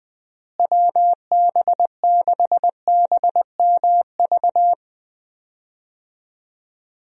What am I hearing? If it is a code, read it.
Morse code, heard as a keyed tone, about 21 wpm, 700 Hz. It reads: WB6BMV
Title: USA Call Signs - 6 Characters - Repeated 2x - 20wpm